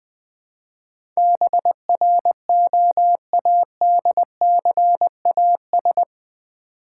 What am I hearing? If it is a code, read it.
Morse code, heard as a keyed tone, about 20 wpm, 700 Hz. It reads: BROADCAS
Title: Build A Word - 4 Letters At A Time - 11-Letter Words - 20wpm